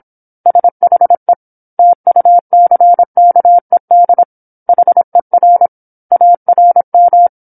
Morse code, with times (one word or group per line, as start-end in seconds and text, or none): 0.5-1.3 SHE
1.8-4.2 TUCKED
4.7-5.7 HER
6.1-7.3 ARM